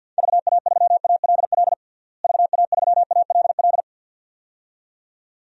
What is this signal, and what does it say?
Morse code, heard as a keyed tone, about 52 wpm, 700 Hz. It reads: VA3ALL VA3ALL
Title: DXCC-100 Call Signs - 6 Characters - Repeated 2x - 50wpm